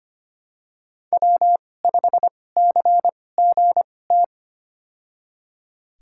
Morse code, heard as a keyed tone, about 25 wpm, spaced as for 18 wpm, 700 Hz. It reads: W5CGT
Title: USA Call Signs - 2x Character Spacing - 25wpm